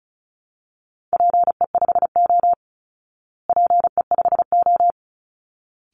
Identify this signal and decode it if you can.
Morse code, heard as a keyed tone, about 34 wpm, 700 Hz. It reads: PE5O PE5O
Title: Call Signs Repeated 3x - 35wpm